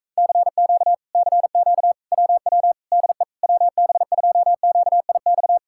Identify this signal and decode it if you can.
Morse code, heard as a keyed tone, about 44 wpm, 700 Hz. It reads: CQ CQ WW DE WB2QIX